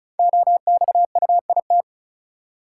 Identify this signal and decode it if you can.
Morse code, heard as a keyed tone, about 36 wpm, 700 Hz. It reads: OXUIT
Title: Code Groups - 5 Letters - 35wpm